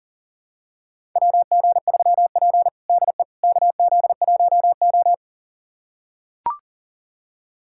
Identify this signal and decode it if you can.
Morse code, heard as a keyed tone, about 40 wpm, 690 Hz. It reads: WG3P DE KZ1O
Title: USA Callsign DE Callsign - 4 Characters - 40wpm